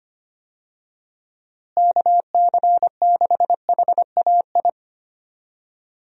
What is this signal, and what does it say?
Morse code, heard as a keyed tone, about 25 wpm, 700 Hz. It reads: KC6HAI
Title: USA Call Signs - 6 Characters - Repeated 3x - 25wpm